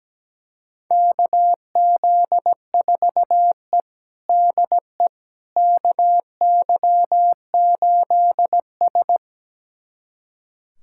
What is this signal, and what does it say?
Morse code, heard as a keyed tone, about 17 wpm, 700 Hz. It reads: KZ4E DE KY8S
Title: USA Callsign DE Callsign - 4 Characters - 17wpm